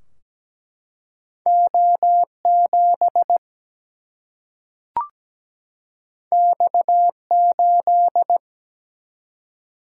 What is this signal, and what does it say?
Morse code, heard as a keyed tone, about 17 wpm, 700 Hz. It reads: O7 X8
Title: Single-Letter Single-Number 17wpm